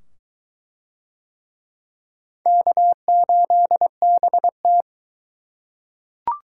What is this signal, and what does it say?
Morse code, heard as a keyed tone, about 22 wpm, 700 Hz. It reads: K8BT